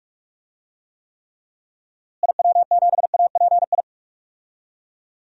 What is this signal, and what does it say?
Morse code, heard as a keyed tone, about 41 wpm, 700 Hz. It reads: IW7API
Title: DXCC-100 Call Signs - 6 Characters - Repeated 3x - 45wpm